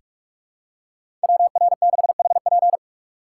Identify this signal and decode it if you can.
Morse code, heard as a keyed tone, about 42 wpm, 700 Hz. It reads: WR6HP
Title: USA Call Signs - 2x Character Spacing - 45wpm